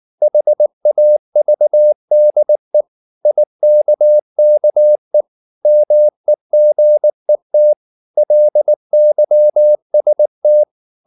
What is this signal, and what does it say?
Morse code, heard as a keyed tone, about 19 wpm, 600 Hz. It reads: HAVDE IKKE MEGET LYST
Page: Ferie 19 wpm